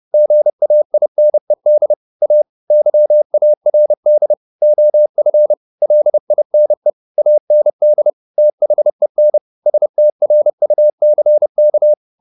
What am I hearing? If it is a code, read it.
Morse code, heard as a keyed tone, about 30 wpm, 600 Hz. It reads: GAINED A YARD OF LINE AND THEN STRUCK
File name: the old man and the sea / the old man and the sea 1 30wpm